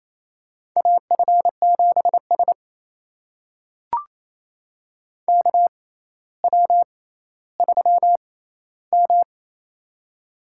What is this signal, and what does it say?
Morse code, heard as a keyed tone, about 28 wpm, spaced as for 14 wpm, 700 Hz. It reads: AF7S KW3M
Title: USA Call Signs - 4 Characters - 6x Character Spacing - 28wpm